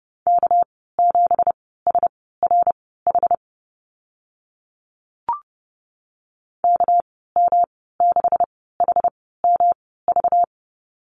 Morse code, heard as a keyed tone, about 30 wpm, spaced as for 17 wpm, 700 Hz. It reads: K7SRH KM6HMV
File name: Lesson 37g - Letter x - Call Signs 30wpm